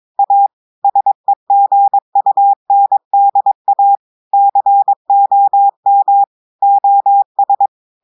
Morse code, heard as a keyed tone, about 22 wpm, 800 Hz. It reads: A SEGUNDA COM OS
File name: audios / portugais / portugais texte en clair 1 vitesse 22 mots minutes